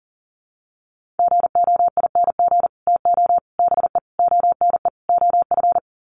Morse code, heard as a keyed tone, about 40 wpm, 700 Hz. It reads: GOING TO BE ONE OF
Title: Sets of 5 Words Encore2 40wpm